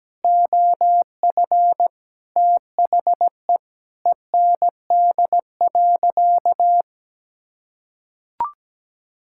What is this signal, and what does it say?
Morse code, heard as a keyed tone, about 17 wpm, 700 Hz. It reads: OF THE END.
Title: Sentences from Top 1000 Words - Review Part 5 - 17wpm